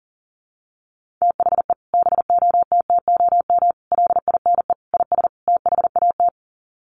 Morse code, heard as a keyed tone, about 40 wpm, 700 Hz. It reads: THE BOTTOM LINE IS THAT
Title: Sets of 5 Words Encore 40wpm